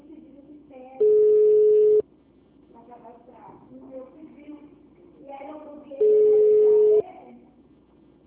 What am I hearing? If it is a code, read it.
Ringback tone (Europe, 425 Hz), with speech.
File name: out- -2001- - - .237